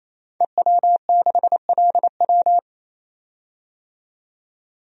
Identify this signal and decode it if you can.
Morse code, heard as a keyed tone, about 28 wpm, 700 Hz. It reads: EW6LW
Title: DXCC-100 Call Signs - 5 Characters - 28wpm